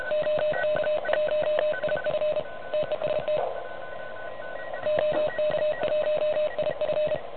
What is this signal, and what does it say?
Morse code, heard as a keyed tone, about 32 wpm, 600 Hz, with noise. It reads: OK1IF TEST OK1IF
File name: truSDX / NZ contest